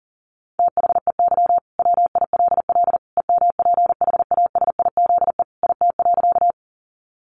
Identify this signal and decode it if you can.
Morse code, heard as a keyed tone, about 40 wpm, 700 Hz. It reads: THEY WILL EMPHASIZE IT.
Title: Sentences from Top 1500 Words 40wpm